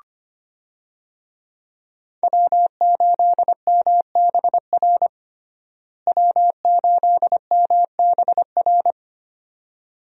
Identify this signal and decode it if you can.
Morse code, heard as a keyed tone, about 25 wpm, 700 Hz. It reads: W8MBR W8MBR
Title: USA Call Signs - 5 Characters - Repeated 2x - 25wpm